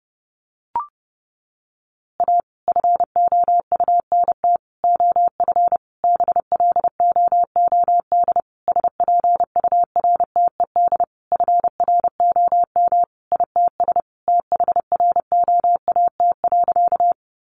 Morse code, heard as a keyed tone, about 30 wpm, 700 Hz. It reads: A FOUNT OF BLOOD SPURTED FROM ITS THROAT.
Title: A Sound Of Thunder 30wpm